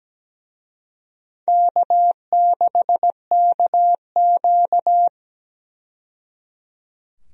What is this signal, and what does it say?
Morse code, heard as a keyed tone, about 17 wpm, 700 Hz. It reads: K6KQ
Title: USA Call Signs - 4 Characters - 17wpm